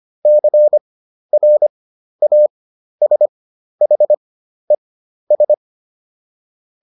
Morse code, heard as a keyed tone, about 25 wpm, spaced as for 12 wpm, 600 Hz. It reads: CRASHES